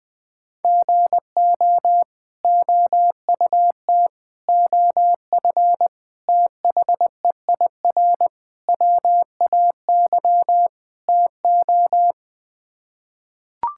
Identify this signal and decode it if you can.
Morse code, heard as a keyed tone, about 20 wpm, 700 Hz. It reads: GO OUT OF THEIR WAY TO
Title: Sets of 6 Words - 20wpm